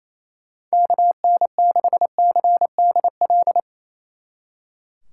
Morse code, heard as a keyed tone, about 31 wpm, 700 Hz. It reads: KN6CDL
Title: Speed Racing - USA 2x3 Call Signs 28wpm